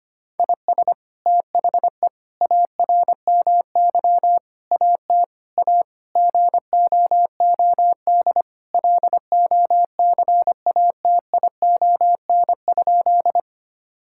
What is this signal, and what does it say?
Morse code, heard as a keyed tone, about 25 wpm, 700 Hz. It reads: IS THE ARMY AT A GOOD LOCATION?
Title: Sentences from Top 1000 Words - Review Part 6 - 25wpm